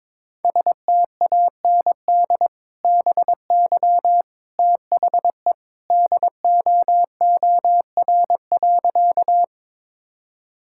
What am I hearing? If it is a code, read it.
Morse code, heard as a keyed tone, about 21 wpm, 700 Hz. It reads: STAND BY THE DOOR.
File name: practice / 22wpm / Sentences with 2-Word Phrases - 22wpm